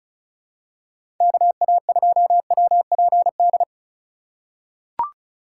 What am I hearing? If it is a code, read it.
Morse code, heard as a keyed tone, about 36 wpm, 700 Hz. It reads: KA2WPD
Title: USA Call Signs - 2x Character Spacing - 35wpm